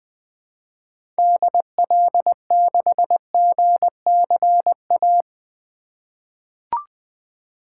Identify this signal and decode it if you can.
Morse code, heard as a keyed tone, about 20 wpm, 700 Hz. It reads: DL6GCA
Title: 2023 SOTA Activator Call Signs - 20wpm